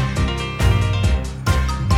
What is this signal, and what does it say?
There is no code.